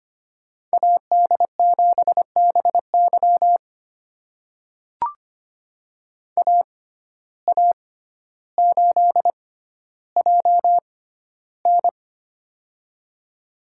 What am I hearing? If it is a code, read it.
Morse code, heard as a keyed tone, about 25 wpm, spaced as for 13 wpm, 700 Hz. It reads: AD7BY AA8JN
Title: USA Call Signs - 5 Characters - 6x Character Spacing - 25wpm